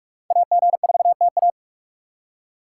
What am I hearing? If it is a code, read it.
Morse code, heard as a keyed tone, about 57 wpm, 700 Hz. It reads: AG4TA